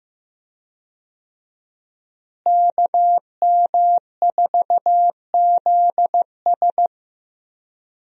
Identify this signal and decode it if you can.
Morse code, heard as a keyed tone, about 16 wpm, 700 Hz. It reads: KM4ZS
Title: USA Call Signs - 5 Characters - Repeated 2x - 15wpm